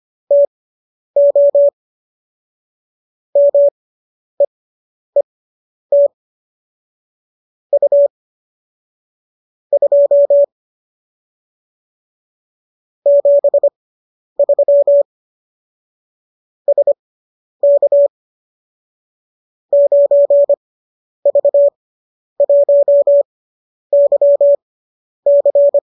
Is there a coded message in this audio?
Yes: Morse code from 0.3 to 25.8 s, heard as a keyed tone, about 25 wpm, spaced as for 10 wpm, 570 Hz. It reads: TO MEET U 2 73 SK 9V1YC